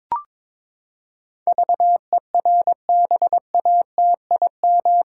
Morse code, heard as a keyed tone, about 22 wpm, 700 Hz. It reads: VERBATIM
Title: Pangram Challenge v2 - 22wpm